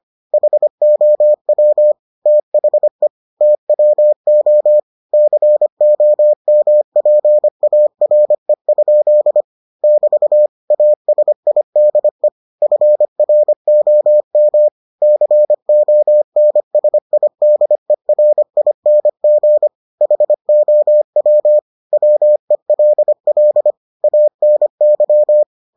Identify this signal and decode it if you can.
Morse code, heard as a keyed tone, about 25 wpm, 600 Hz. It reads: HOW THE TWO COMPARE? <BT> ASIDE FROM CONSIDERING HOW WELL ANY